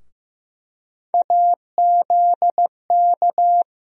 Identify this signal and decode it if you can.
Morse code, heard as a keyed tone, about 16 wpm, 700 Hz. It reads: AZK
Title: USA Call Sign Postfixes - 3 Letters 15wpm